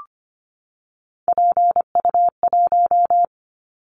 Morse code, heard as a keyed tone, about 25 wpm, 700 Hz. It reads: PU1
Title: Callsign Prefixes for Top 100 DXCC - 25wpm